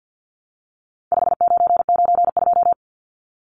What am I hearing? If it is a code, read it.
Morse code, heard as a keyed tone, about 50 wpm, 700 Hz. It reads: 5992
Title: Numbers - 4-Digit 50wpm